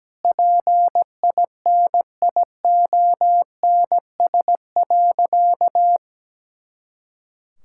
Morse code, heard as a keyed tone, about 17 wpm, 700 Hz. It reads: PINIONS.
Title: Pangram Challenge v2 - 17wpm